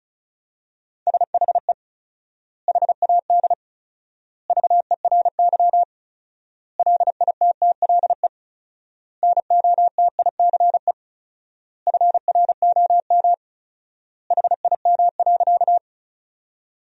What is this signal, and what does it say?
Morse code, heard as a keyed tone, about 36 wpm, 700 Hz. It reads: SHE HAD VERY LITTLE NOTICE FROM HIM.
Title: 4x Word-Spacing - Sentences from Top 400 Words - 35wpm